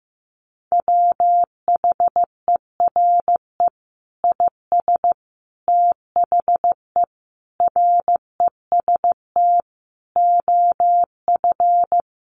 Morse code, heard as a keyed tone, about 15 wpm, 700 Hz. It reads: WHERE IS THE REST OF
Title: Sentences from Top 1500 Words 15wpm